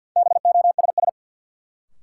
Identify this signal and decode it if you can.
Morse code, heard as a keyed tone, about 48 wpm, 700 Hz. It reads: BOSS
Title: Sentences from Top 700 Words - New Word Drill - 50wpm